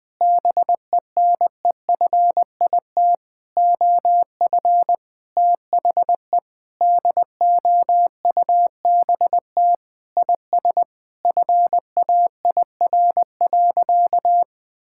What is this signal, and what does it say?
Morse code, heard as a keyed tone, about 20 wpm, 700 Hz. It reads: BENEFIT OF THE DOUBT IS FAIR.
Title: Sentences with 8-Word Phrases - 20wpm